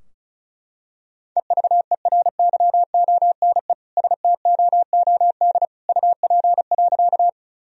Morse code, heard as a keyed tone, about 34 wpm, 700 Hz. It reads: EVERYONE STOOD UP.
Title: Sentences from Top 1000 Words - Review Part 2 - 35wpm